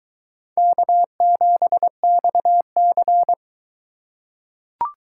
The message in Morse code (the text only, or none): K7XC